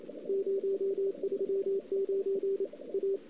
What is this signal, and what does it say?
Morse code, heard as a keyed tone, about 31 wpm, 410 Hz, with radio noise. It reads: 039 A
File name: CW / 2008-ss-cw-k5tr-op-n5rz -02:00